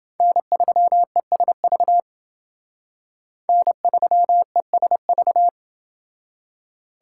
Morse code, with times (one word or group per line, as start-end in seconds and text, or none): 0.2-2.0 N3ESV
3.5-5.5 N3ESV